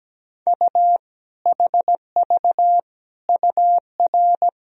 Morse code, heard as a keyed tone, about 17 wpm, 700 Hz. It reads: U HV UR
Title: Sentences from Top 400 Words and Abbreviations 17wpm